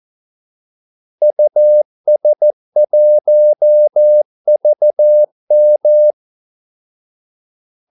Morse code, heard as a keyed tone, about 14 wpm, 600 Hz. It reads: US1VM
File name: Callsigns-14wpm